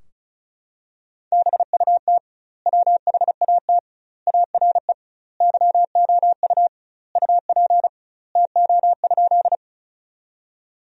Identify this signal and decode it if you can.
Morse code, heard as a keyed tone, about 36 wpm, 700 Hz. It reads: BUT WHAT ARE YOU UP TO?
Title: 2x Word-Spacing - Sentences from Top 100 Words - 35wpm